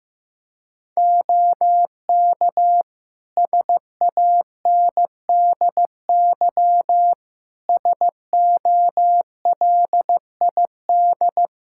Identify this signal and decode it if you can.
Morse code, heard as a keyed tone, about 15 wpm, 700 Hz. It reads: OK SANDY SOLID
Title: Practice QSO - Exchange 3 - More Info - 15wpm